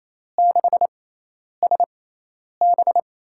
Morse code, heard as a keyed tone, about 28 wpm, spaced as for 10 wpm, 700 Hz. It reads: 6SB
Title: USA Call Signs - 6x Character Spacing - 28wpm